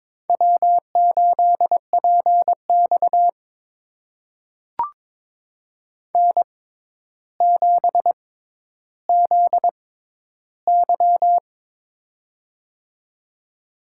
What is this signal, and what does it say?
Morse code, heard as a keyed tone, about 21 wpm, spaced as for 11 wpm, 700 Hz. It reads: W8PX N7ZY
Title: USA Call Signs - 4 Characters - 6x Character Spacing - 22wpm